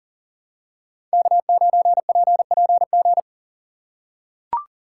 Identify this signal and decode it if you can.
Morse code, heard as a keyed tone, about 38 wpm, 700 Hz. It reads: K9PPG
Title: USA Call Signs - 5 Characters - 3x Character Spacing - 40wpm